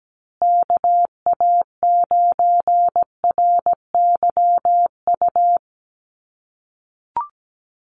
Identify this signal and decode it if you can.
Morse code, heard as a keyed tone, about 17 wpm, 700 Hz. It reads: KA9RYU
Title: USA Call Signs Repeated 2x 17wpm